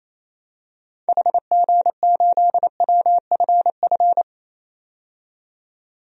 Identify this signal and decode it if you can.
Morse code, heard as a keyed tone, about 28 wpm, 700 Hz. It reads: HG8WFF
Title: DXCC-100 Call Signs - 6 Characters - 28wpm